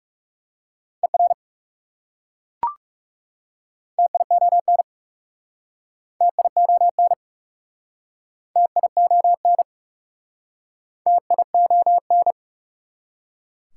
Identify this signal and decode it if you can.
Morse code, heard as a keyed tone, about 37 wpm, 700 Hz. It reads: ER TION TION TION TION
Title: Speed-Racing Prefixes and Suffixes 30wpm